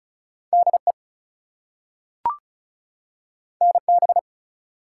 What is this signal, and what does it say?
Morse code, heard as a keyed tone, about 36 wpm, 700 Hz. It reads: DE NB
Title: US States and Canadian Provinces 35wpm